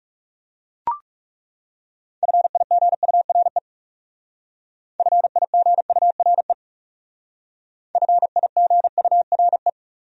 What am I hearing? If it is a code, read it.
Morse code, heard as a keyed tone, about 36 wpm, 700 Hz. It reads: FIGURE FIGURE FIGURE
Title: Speed-Racing Top 500 Words 30wpm